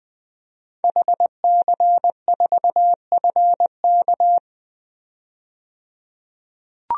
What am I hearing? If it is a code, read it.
Morse code, heard as a keyed tone, about 20 wpm, 700 Hz. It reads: HC4FK
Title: DXCC-100 Call Signs - 5 Characters - 20wpm